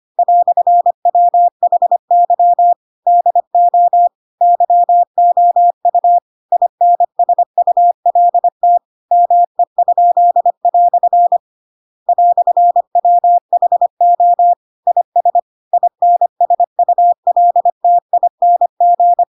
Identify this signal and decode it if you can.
Morse code, heard as a keyed tone, about 25 wpm, 700 Hz. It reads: "WHY DO YOU INSULT ME?" "WHO IS INSULTING
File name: Pinocchio / Pinocchio ch2 25 25 ik1hge